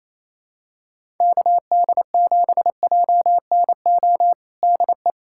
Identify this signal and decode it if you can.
Morse code, heard as a keyed tone, about 28 wpm, 700 Hz. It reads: KD7JNO DE
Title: USA Callsign DE Callsign 28wpm